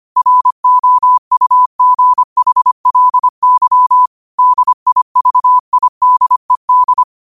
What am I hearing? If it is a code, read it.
Morse code, heard as a keyed tone, about 25 wpm, 1000 Hz. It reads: ROUGHLY DIVIDED
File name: qotdfiles / 25wpm